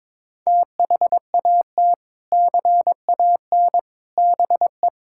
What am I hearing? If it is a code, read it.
Morse code, heard as a keyed tone, about 22 wpm, 700 Hz. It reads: THAT CAN BE
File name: practice / 22wpm / Sentences with 3-Word Phrases - 22wpm